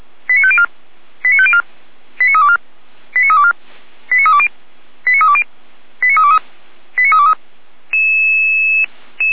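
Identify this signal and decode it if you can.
Five-tone selective call (ZVEI), heard as quick runs of five tones, tones of about 70 ms. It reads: 89504 89504 89324 89324 89320 89320 89323 89323
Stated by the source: ZVEI five-tone selective call received on SDR, logged by feuerwehr-orlamuende fire brigade server.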